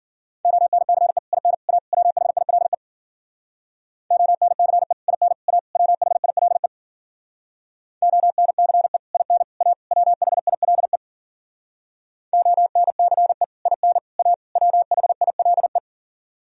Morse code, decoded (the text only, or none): ONCE IN A WHILE ONCE IN A WHILE ONCE IN A WHILE ONCE IN A WHILE